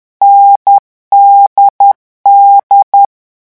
Morse code, heard as a keyed tone, about 10 wpm, 800 Hz. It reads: NDD
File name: morse mp3 / lecon4-vitesse 10